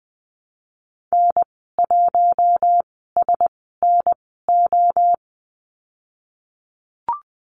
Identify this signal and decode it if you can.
Morse code, heard as a keyed tone, about 19 wpm, spaced as for 14 wpm, 700 Hz. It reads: N1SNO
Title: 076 Lesson 13g - Letter L - Call Signs 20wpm